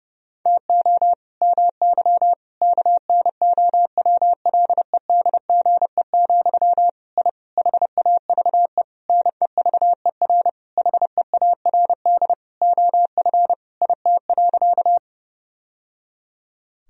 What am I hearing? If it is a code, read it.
Morse code, heard as a keyed tone, about 30 wpm, 700 Hz. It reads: TO MY KNOWLEDGE, I HAVE NEVER HEARD OF IT.
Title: Sentences from Top 1000 Words - Review Part 3 - 30wpm